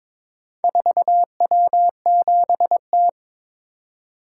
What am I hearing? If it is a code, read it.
Morse code, heard as a keyed tone, about 22 wpm, 700 Hz. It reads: 4W7T
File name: practice / 22wpm / Four Letter-Number Combinations - 22wpm